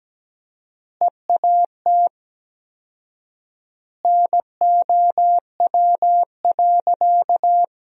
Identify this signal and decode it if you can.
Morse code, heard as a keyed tone, about 17 wpm, 700 Hz. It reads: EAT NOW.
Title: 4x Word-Spacing - Sentences from Top 300 Words - 17wpm